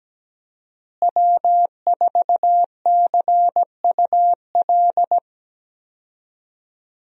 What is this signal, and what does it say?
Morse code, heard as a keyed tone, about 17 wpm, 700 Hz. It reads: W4CUL
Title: USA Call Signs - 5 Characters - Repeated 2x - 17wpm